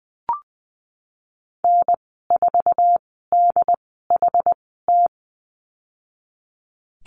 Morse code, heard as a keyed tone, about 20 wpm, spaced as for 15 wpm, 700 Hz. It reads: N4DHT
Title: 082 Lesson 14g - Number 2 - Call Signs 20wpm